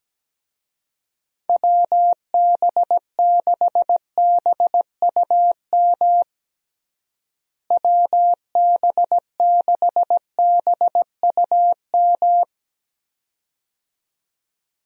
Morse code, heard as a keyed tone, about 17 wpm, 700 Hz. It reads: WB6BUM WB6BUM
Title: USA Call Signs - 6 Characters - Repeated 2x - 17wpm